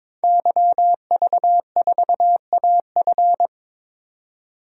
Morse code, heard as a keyed tone, about 22 wpm, 700 Hz. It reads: YV4AF
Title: DXCC-100 Call Signs - 5 Characters - 22wpm